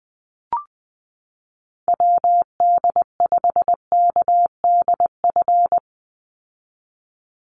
Morse code, heard as a keyed tone, about 20 wpm, 700 Hz. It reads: WD5KDF
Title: USA Call Signs 20wpm